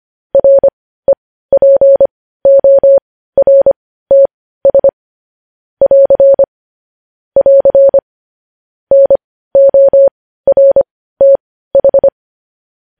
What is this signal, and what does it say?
Morse code, heard as a keyed tone, about 25 wpm, spaced as for 15 wpm, 550 Hz. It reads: REPORTS <AR> <AR> NORTH
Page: News headlines in Morse code at 15 wpm